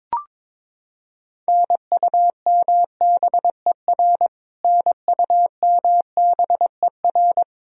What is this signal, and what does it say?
Morse code, heard as a keyed tone, about 22 wpm, 700 Hz. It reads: NUMBER NUMBER
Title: Top 100 Words Repeated 22wpm